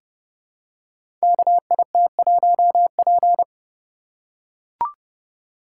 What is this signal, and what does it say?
Morse code, heard as a keyed tone, about 30 wpm, 700 Hz. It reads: KIT1P
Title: Code Groups - 5 Letter:Numbers - 30wpm